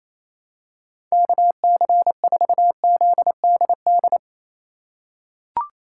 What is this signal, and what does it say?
Morse code, heard as a keyed tone, about 28 wpm, 700 Hz. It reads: KC4ZDD
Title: USA Call Signs - 2x Character Spacing - 28wpm